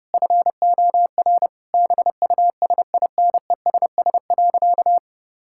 Morse code, heard as a keyed tone, about 30 wpm, 700 Hz. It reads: FOR BUSINESS.
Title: Sentences from Top 1000 Words - Review Part 6 - 30wpm